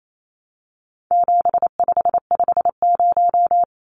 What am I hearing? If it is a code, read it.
Morse code, heard as a keyed tone, about 28 wpm, 700 Hz. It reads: 7550
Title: Numbers - 4-Digit 28wpm